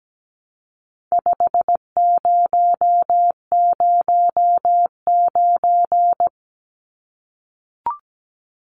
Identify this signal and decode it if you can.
Morse code, heard as a keyed tone, about 17 wpm, 700 Hz. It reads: 5009
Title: Numbers - 4-Digit 17wpm